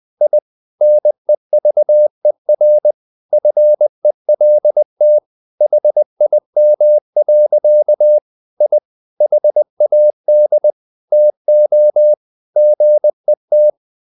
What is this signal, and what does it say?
Morse code, heard as a keyed tone, about 20 wpm, 600 Hz. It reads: I NEVER FELT HIM. I HAD TO GET
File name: the old man and the sea / the old man and the sea part 1 20wpm